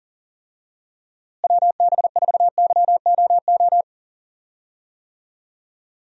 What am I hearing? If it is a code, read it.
Morse code, heard as a keyed tone, about 39 wpm, 700 Hz. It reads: WB4YOO
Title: USA Call Signs - 6 Characters - Repeated 3x - 40wpm